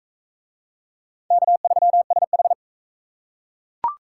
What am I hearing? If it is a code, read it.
Morse code, heard as a keyed tone, about 44 wpm, 700 Hz. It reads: K3SH